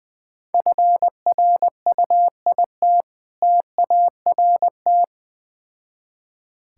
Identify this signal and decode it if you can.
Morse code, heard as a keyed tone, about 20 wpm, 690 Hz. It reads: FRUIT TART
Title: Difficult Words with Hints - 20wpm